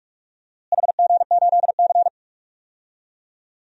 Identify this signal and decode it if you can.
Morse code, heard as a keyed tone, about 46 wpm, 700 Hz. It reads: HG8C
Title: DXCC-100 Call Signs - 4 Characters - Repeated 3x - 45wpm